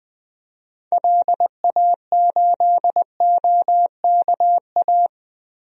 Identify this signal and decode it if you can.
Morse code, heard as a keyed tone, about 20 wpm, 700 Hz. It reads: LA8OKA
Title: DXCC-100 Call Signs - 6 Characters - 20wpm